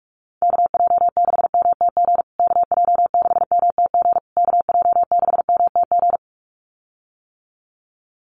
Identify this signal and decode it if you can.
Morse code, heard as a keyed tone, about 44 wpm, 700 Hz. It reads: KJ6MTG KJ6MTG KJ6MTG
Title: USA Call Signs Repeated 3x 45wpm